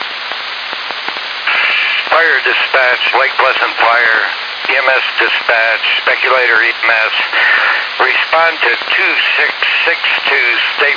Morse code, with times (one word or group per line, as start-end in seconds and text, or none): none